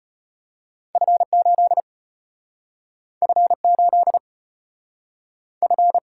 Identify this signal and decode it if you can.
Morse code, heard as a keyed tone, about 34 wpm, 700 Hz. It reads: F 8 F 8 F